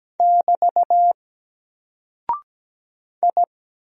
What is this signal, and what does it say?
Morse code, heard as a keyed tone, about 17 wpm, 700 Hz. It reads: <BT> I
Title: Single Character (Letter:Number:Punctuation:Prosign) Alt 17wpm